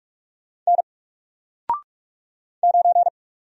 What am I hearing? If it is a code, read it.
Morse code, heard as a keyed tone, about 44 wpm, 700 Hz. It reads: N 9
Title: Single Letter-Number 45wpm